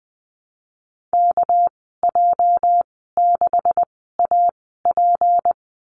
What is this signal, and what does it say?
Morse code, heard as a keyed tone, about 20 wpm, spaced as for 14 wpm, 700 Hz. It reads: KJ6AP
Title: 245 Lesson 39g - Letter z - Call Signs 20wpm